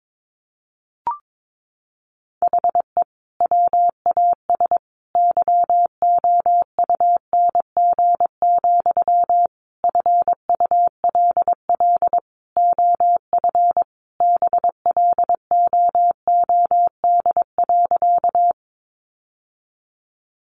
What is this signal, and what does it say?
Morse code, heard as a keyed tone, about 22 wpm, 700 Hz. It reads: HE WAS YOUNG, FULL OF BLOOD.
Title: Sentences from Top 1100 Words 22wpm